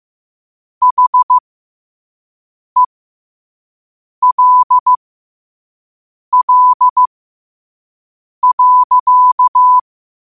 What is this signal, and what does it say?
Morse code, heard as a keyed tone, about 15 wpm, spaced as for 5 wpm, 1000 Hz. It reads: HELL.
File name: qotdfiles / 5wpm